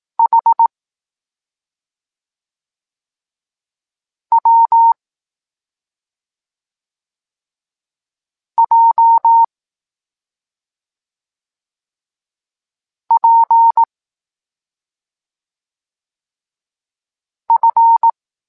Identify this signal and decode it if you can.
Morse code, heard as a keyed tone, about 18 wpm, 920 Hz. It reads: H W J P F
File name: Tekst-26 1